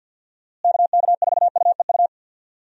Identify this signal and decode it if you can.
Morse code, heard as a keyed tone, about 51 wpm, 700 Hz. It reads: KK4UEU
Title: USA Call Signs - 6 Characters - Repeated 2x - 50wpm